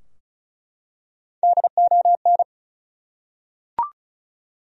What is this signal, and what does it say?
Morse code, heard as a keyed tone, about 35 wpm, 700 Hz. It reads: DON
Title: USA Common Names - Max 5 Letters - 35wpm